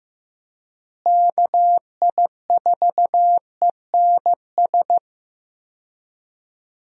Morse code, heard as a keyed tone, about 15 wpm, 700 Hz. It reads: KI4ENS
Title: USA Call Signs - 6 Characters - Repeated 3x - 15wpm